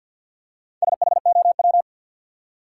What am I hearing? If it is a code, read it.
Morse code, heard as a keyed tone, about 53 wpm, 700 Hz. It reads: SHOW
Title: Four Letter Words - 50wpm